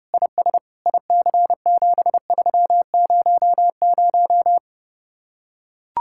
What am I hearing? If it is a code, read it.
Morse code, heard as a keyed tone, about 30 wpm, 700 Hz. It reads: IS IC7300